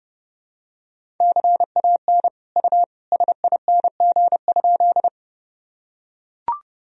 Morse code, heard as a keyed tone, about 30 wpm, 700 Hz. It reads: CAN U SING?
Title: Sentences from Top 400 Words and Abbreviations 30wpm